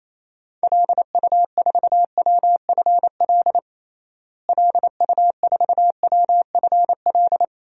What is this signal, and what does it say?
Morse code, heard as a keyed tone, about 27 wpm, 700 Hz. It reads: LU4WFL LU4WFL
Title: DXCC-100 Call Signs - 6 Characters - Repeated 2x - 28wpm